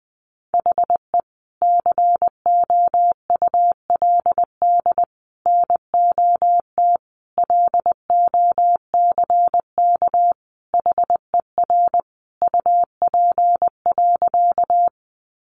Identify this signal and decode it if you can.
Morse code, heard as a keyed tone, about 20 wpm, 700 Hz. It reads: HE COULD NOT LOCK HER UP.
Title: Sentences from Top 1300 Words 20wpm